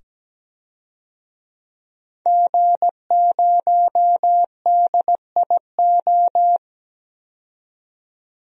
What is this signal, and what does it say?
Morse code, heard as a keyed tone, about 17 wpm, 700 Hz. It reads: G0DIO
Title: DXCC-100 Call Signs - 5 Characters - 17wpm